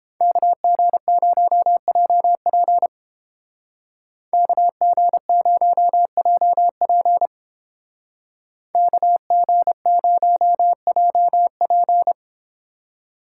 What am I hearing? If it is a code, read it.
Morse code, heard as a keyed tone, about 29 wpm, 700 Hz. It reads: KG0JP KG0JP KG0JP